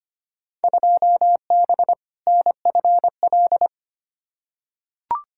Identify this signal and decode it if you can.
Morse code, heard as a keyed tone, about 25 wpm, 700 Hz. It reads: 2B NFL
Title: Field Day Exchanges Encore - 25wpm